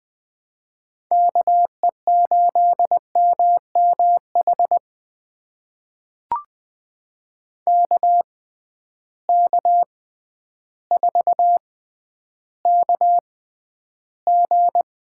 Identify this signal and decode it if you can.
Morse code, heard as a keyed tone, about 21 wpm, spaced as for 11 wpm, 700 Hz. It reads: KE8MMH KK4KG
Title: USA Call Signs - 6 Characters - 6x Character Spacing - 20wpm